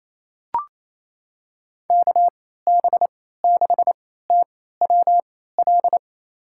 Morse code, heard as a keyed tone, about 28 wpm, spaced as for 16 wpm, 700 Hz. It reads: KB6TWL
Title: USA Call Signs - 3x Character Spacing - 28wpm